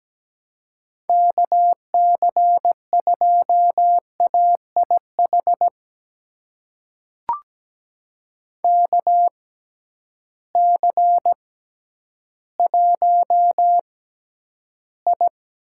Morse code, heard as a keyed tone, about 17 wpm, spaced as for 10 wpm, 700 Hz. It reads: KC2AIH KC1I
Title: USA Call Signs - 6 Characters - 6x Character Spacing - 17wpm